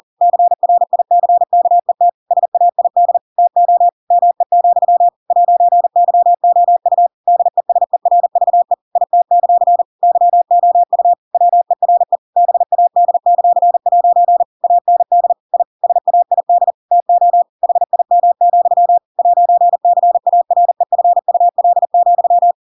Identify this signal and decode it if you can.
Morse code, heard as a keyed tone, about 40 wpm, 700 Hz. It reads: CRICKET SAID TO ME, 'YOU DESERVE IT; YOU WERE BAD;' AND I SAID TO HIM, 'CAREFUL,
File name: Pinocchio / Pinocchio ch7 40 40 ik1hge